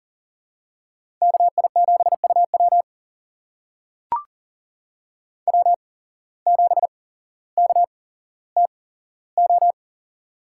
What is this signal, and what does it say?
Morse code, heard as a keyed tone, about 40 wpm, spaced as for 17 wpm, 700 Hz. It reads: KI7UW W7KTO
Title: USA Call Signs - 5 Characters - 8x Character Spacing - 40wpm